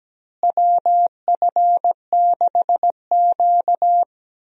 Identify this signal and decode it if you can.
Morse code, heard as a keyed tone, about 17 wpm, 700 Hz. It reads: WF6Q
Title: USA Call Signs - 4 Characters - Repeated 3x - 17wpm